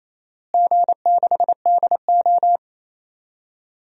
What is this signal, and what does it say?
Morse code, heard as a keyed tone, about 28 wpm, 700 Hz. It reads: G6DO
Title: DXCC-100 Call Signs - 4 Characters - Repeated 3x - 28wpm